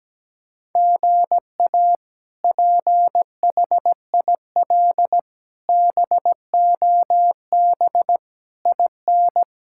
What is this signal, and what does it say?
Morse code, heard as a keyed tone, about 17 wpm, 700 Hz. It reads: GA PHIL BOB IN
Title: K1USN SST QSO Practice - 17wpm